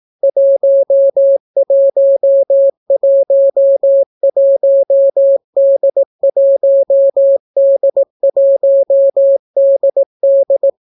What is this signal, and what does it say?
Morse code, heard as a keyed tone, about 18 wpm, 550 Hz. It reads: 1111D1D1DD